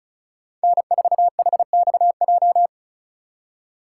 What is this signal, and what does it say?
Morse code, heard as a keyed tone, about 36 wpm, 700 Hz. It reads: N4HXJ
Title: USA Call Signs - 5 Characters - 2x Character Spacing - 35wpm